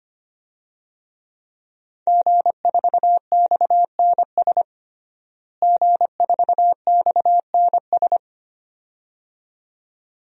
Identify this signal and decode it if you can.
Morse code, heard as a keyed tone, about 25 wpm, 700 Hz. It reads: G4XNS G4XNS
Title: DXCC-100 Call Signs - 5 Characters - Repeated 2x - 25wpm